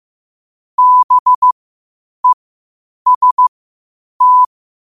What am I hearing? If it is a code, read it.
Morse code, heard as a keyed tone, about 15 wpm, spaced as for 8 wpm, 1000 Hz. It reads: BEST
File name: qotdfiles / 8wpm